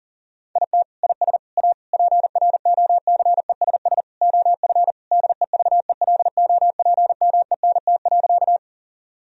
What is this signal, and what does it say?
Morse code, heard as a keyed tone, about 40 wpm, 700 Hz. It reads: IT IS A PROCESS OF DEVELOPMENT.
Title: Sentences from Top 1000 Words - Review Part 4 - 40wpm